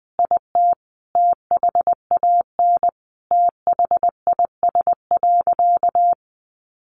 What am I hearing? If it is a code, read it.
Morse code, heard as a keyed tone, about 20 wpm, 700 Hz. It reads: IT THAN THIS.
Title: Sentences from Top 100 Words 20wpm